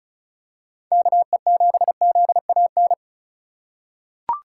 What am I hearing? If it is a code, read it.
Morse code, heard as a keyed tone, about 36 wpm, 700 Hz. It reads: KE7ZAN
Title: USA Call Signs - 6x Character Spacing - 35wpm